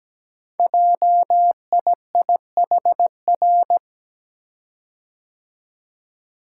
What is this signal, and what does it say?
Morse code, heard as a keyed tone, about 17 wpm, 700 Hz. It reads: JIIHR
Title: Code Groups - 5 Letters - 17wpm